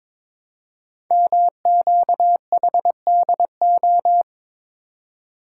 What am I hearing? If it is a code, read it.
Morse code, heard as a keyed tone, about 22 wpm, 700 Hz. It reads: MQHDO
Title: Code Groups - 5 Letters - 22wpm